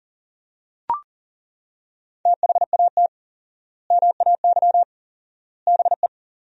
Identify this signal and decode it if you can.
Morse code, heard as a keyed tone, about 37 wpm, 700 Hz. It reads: THAT MAY BE
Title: 4x Word-Spacing - Sentences from Top 400 Words - 40wpm